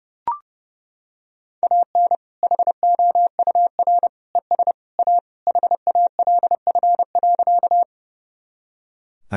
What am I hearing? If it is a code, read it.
Morse code, heard as a keyed tone, about 32 wpm, 700 Hz. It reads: AN HOUR ES A HALF.
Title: Sentences from Top 400 Words and Abbreviations 30wpm